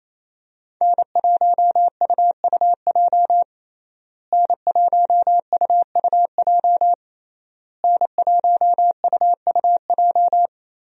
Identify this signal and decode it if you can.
Morse code, heard as a keyed tone, about 28 wpm, 700 Hz. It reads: N1UUJ N1UUJ N1UUJ
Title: USA Call Signs - 5 Characters - Repeated 3x - 28wpm